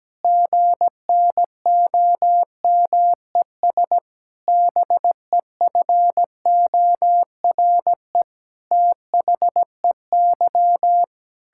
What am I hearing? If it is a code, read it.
Morse code, heard as a keyed tone, about 17 wpm, 700 Hz. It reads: GNOMES BEFORE THEY
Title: Pangram Challenge v2 - 17wpm